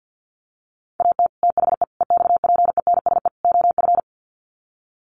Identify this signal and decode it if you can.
Morse code, heard as a keyed tone, about 50 wpm, 700 Hz. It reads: AT THE EXPENSE OF
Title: Sets of 4 Words 50wpm